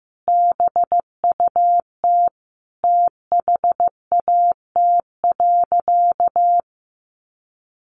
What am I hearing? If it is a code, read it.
Morse code, heard as a keyed tone, about 15 wpm, 700 Hz. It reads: BUT THAT.
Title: Sentences from Top 1000 Words 15wpm